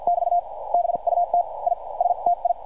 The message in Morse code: <BT> BUT I STI